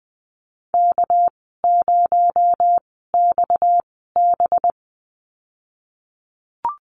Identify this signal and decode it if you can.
Morse code, heard as a keyed tone, about 20 wpm, spaced as for 14 wpm, 700 Hz. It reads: K0XB
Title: 245 Lesson 39g - Letter z - Call Signs 20wpm